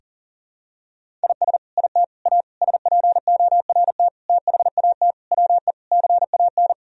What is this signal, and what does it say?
Morse code, heard as a keyed tone, about 40 wpm, 690 Hz. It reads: IS IT A SPORT THAT WE CAN
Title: Sentences from Top 1000 Words - Review Part 4 - 40wpm